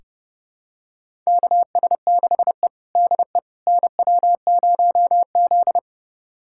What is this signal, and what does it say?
Morse code, heard as a keyed tone, about 32 wpm, 700 Hz. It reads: KS6E DE NW0Z
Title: USA Callsign DE Callsign - 4 Characters - 30wpm